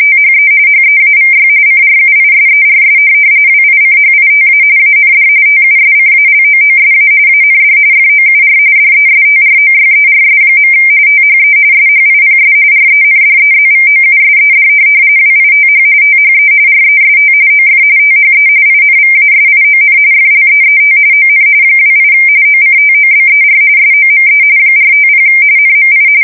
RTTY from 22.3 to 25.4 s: OCTOBER SOME 450